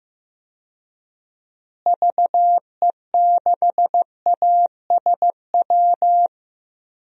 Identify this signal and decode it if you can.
Morse code, heard as a keyed tone, about 15 wpm, 700 Hz. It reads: VE6ASW
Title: DXCC-100 Call Signs - 6 Characters - Repeated 2x - 15wpm